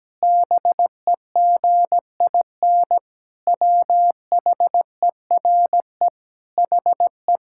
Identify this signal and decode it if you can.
Morse code, heard as a keyed tone, about 17 wpm, 700 Hz. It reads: BEGIN WHERE HE
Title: Sentences from Top 1000 Words - Review Part 6 - 17wpm